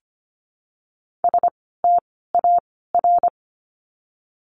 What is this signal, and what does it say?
Morse code, heard as a keyed tone, about 25 wpm, 700 Hz. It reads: S T A R
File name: course / 25:10wpm / 094 Lesson 16f - Letter c - Words 25wpm